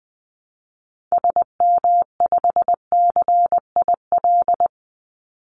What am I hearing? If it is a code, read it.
Morse code, heard as a keyed tone, about 20 wpm, 700 Hz. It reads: SM5CIL
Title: Call Signs - 20wpm